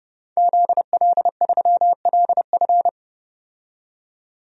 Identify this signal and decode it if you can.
Morse code, heard as a keyed tone, about 30 wpm, 700 Hz. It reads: ZL3LF
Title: DXCC-100 Call Signs - 5 Characters - 30wpm